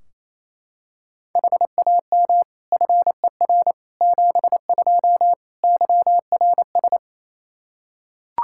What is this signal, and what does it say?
Morse code, heard as a keyed tone, about 27 wpm, 700 Hz. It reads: HAM FER 72 YRS
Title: Practice QSO - Exchange 3 - More Info - 28wpm